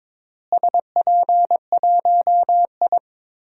Morse code, heard as a keyed tone, about 22 wpm, 700 Hz. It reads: SP1I
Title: DXCC-100 Call Signs - 4 Characters - Repeated 2x - 22wpm